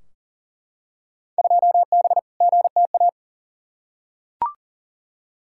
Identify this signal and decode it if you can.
Morse code, heard as a keyed tone, about 40 wpm, 700 Hz. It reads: 2B GTA
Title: Field Day Exchanges Encore - 40wpm